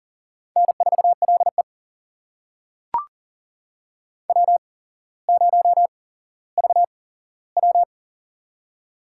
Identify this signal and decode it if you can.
Morse code, heard as a keyed tone, about 40 wpm, spaced as for 17 wpm, 700 Hz. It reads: N4LE W0VW
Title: USA Call Signs - 4 Characters - 8x Character Spacing - 40wpm